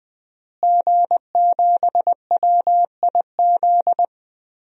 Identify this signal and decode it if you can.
Morse code, heard as a keyed tone, about 21 wpm, 700 Hz. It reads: G7WIZ
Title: DXCC-100 Call Signs - 5 Characters - Repeated 3x - 20wpm